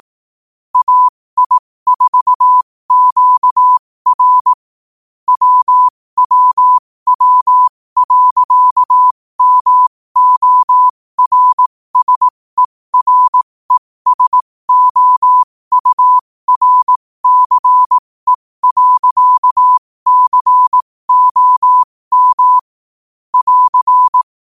Morse code, heard as a keyed tone, about 18 wpm, spaced as for 15 wpm, 1000 Hz. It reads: AI4QR WWW.MORSERESOURCE.COM <AR>
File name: qotdfiles / 15wpm